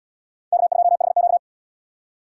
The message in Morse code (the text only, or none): K2HQ